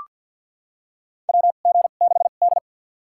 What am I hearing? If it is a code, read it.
Morse code, heard as a keyed tone, about 46 wpm, spaced as for 34 wpm, 700 Hz. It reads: WG6D
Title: USA Call Signs - 2x Character Spacing - 50wpm